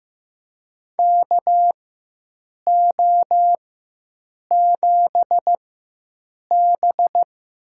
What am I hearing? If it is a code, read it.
Morse code, heard as a keyed tone, about 15 wpm, spaced as for 7 wpm, 700 Hz. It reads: KO7B
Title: USA Call Signs - 4 Characters - 4x Character Spacing - 15wpm